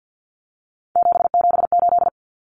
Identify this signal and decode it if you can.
Morse code, heard as a keyed tone, about 50 wpm, 700 Hz. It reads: 778
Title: Numbers - 3-Digit 50wpm